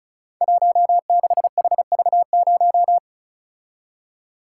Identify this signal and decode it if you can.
Morse code, heard as a keyed tone, about 36 wpm, 700 Hz. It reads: 16HV0
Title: Code Groups - 5 Letter:Numbers - 35wpm